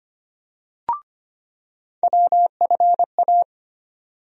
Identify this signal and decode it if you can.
Morse code, heard as a keyed tone, about 25 wpm, 700 Hz. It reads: WFA
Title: USA Call Sign Postfixes - 3 Letters 25wpm